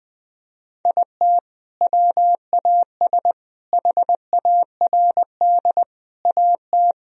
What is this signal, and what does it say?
Morse code, heard as a keyed tone, about 20 wpm, 700 Hz. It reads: IT WAS HARD AT
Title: Sentences from Top 1000 Words - Review Part 1 - 20wpm